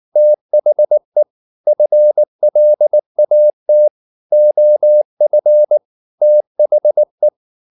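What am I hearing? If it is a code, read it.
Morse code, heard as a keyed tone, about 19 wpm, 600 Hz. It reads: THE FLAT OF THE
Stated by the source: Morse code at 19 wpm from CW med Gnister, reading Ken Follett's novel Never in English.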